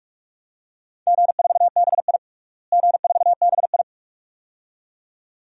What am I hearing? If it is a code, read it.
Morse code, heard as a keyed tone, about 46 wpm, 700 Hz. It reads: G4BI G4BI
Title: DXCC-100 Call Signs - 4 Characters - Repeated 2x - 45wpm